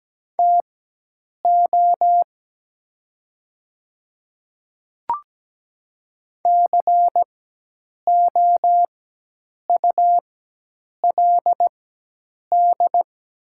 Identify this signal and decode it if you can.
Morse code, heard as a keyed tone, about 17 wpm, spaced as for 8 wpm, 700 Hz. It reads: TO COULD
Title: 4x Character Spacing - Top 100 Words - 17wpm